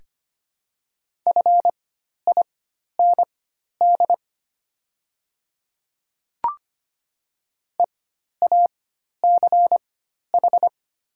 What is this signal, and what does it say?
Morse code, heard as a keyed tone, about 25 wpm, spaced as for 12 wpm, 700 Hz. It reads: FIND EACH
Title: 4x Character Spacing - Top 100 Words - 25wpm